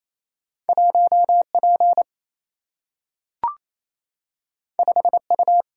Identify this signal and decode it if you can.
Morse code, heard as a keyed tone, about 28 wpm, 700 Hz. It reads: 1P 5U
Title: Single-Letter Single-Number 28wpm